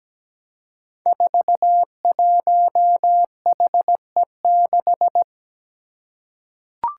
Morse code, heard as a keyed tone, about 17 wpm, 700 Hz. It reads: 41HE6
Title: Five Challenging Letter-Number Combinations 17wpm